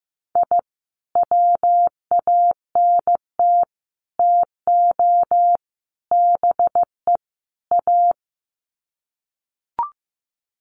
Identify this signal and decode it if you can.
Morse code, heard as a keyed tone, about 15 wpm, 700 Hz. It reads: I WANT TO BE A
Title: Sets of 5 Words Encore 15wpm